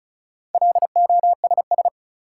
Morse code, heard as a keyed tone, about 36 wpm, 700 Hz. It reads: LOSS
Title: Sentences from Top 1400 Words - New Word Drill - 35wpm